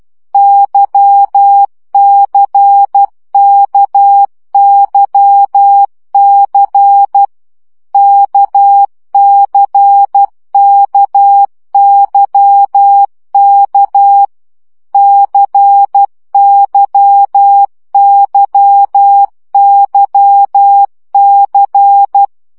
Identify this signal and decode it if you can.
Morse code, heard as a keyed tone, about 12 wpm, 800 Hz. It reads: YCKYC KCKYK CYYYC
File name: CW gyakorlas k c y 10 baud